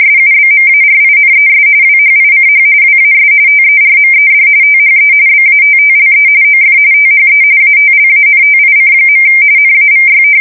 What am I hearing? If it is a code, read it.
RTTY, about 45.45 baud. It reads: SUBMARINES WHICH WAS TERMED THE BATTLE OF THE ATLANTIC. NAVY
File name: fsk170 / usnavy4